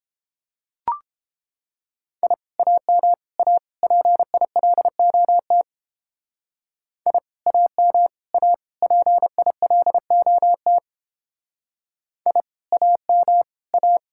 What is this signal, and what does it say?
Morse code, heard as a keyed tone, about 29 wpm, 700 Hz. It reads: I AM A PILOT I AM A PILOT I AM A